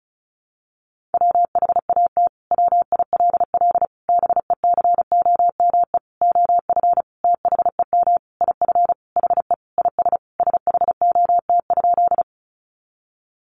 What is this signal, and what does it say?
Morse code, heard as a keyed tone, about 34 wpm, 700 Hz. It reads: WHAT WILL BECOME OF THEM IF HE IS SHOT?
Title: Sentences from Top 1100 Words 35wpm